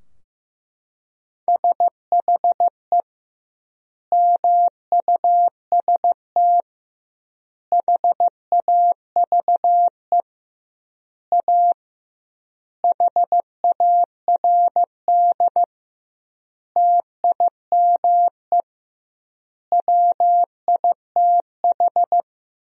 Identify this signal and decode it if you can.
Morse code, heard as a keyed tone, about 15 wpm, 700 Hz. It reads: SHE MUST HAVE A HARD TIME WITH
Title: 2x Word-Spacing - Sentences from Top 300 Words - 15wpm